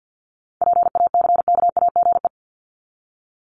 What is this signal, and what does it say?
Morse code, heard as a keyed tone, about 50 wpm, 700 Hz. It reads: PACKAGE
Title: Top 1501-1750 Words 50wpm